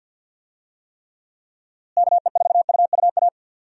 Morse code, heard as a keyed tone, about 52 wpm, 700 Hz. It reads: KE4UUA
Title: USA Call Signs - 6 Characters - Repeated 2x - 50wpm